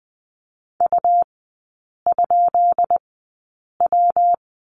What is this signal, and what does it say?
Morse code, heard as a keyed tone, about 20 wpm, spaced as for 8 wpm, 700 Hz. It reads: U?W
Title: 133 Lesson 22e - Question Mark - Three Characters 20wpm